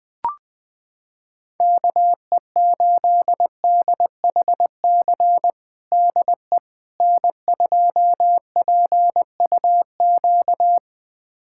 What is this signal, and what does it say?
Morse code, heard as a keyed tone, about 20 wpm, 690 Hz. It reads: KE8DHC DE N2PUQ
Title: USA Callsign DE Callsign 20wpm